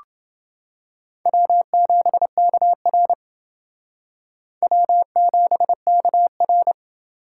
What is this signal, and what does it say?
Morse code, heard as a keyed tone, about 28 wpm, 700 Hz. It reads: W7KR W7KR